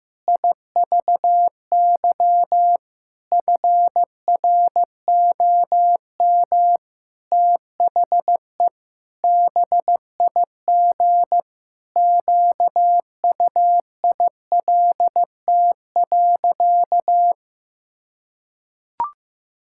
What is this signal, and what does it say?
Morse code, heard as a keyed tone, about 15 wpm, 700 Hz. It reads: IVY FROM THE BIG QUILT.
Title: Pangram Challenge v2 - 15wpm